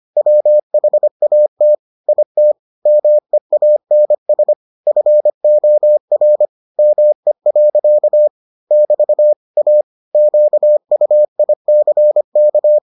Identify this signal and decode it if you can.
Morse code, heard as a keyed tone, about 25 wpm, 600 Hz. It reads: WHAT IT MEANS FOR ME. <BT> A QUICK